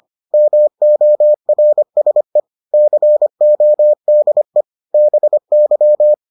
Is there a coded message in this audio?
Yes: Morse code from 0.3 to 6.1 s, heard as a keyed tone, about 25 wpm, 600 Hz. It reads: MORSE CODE BY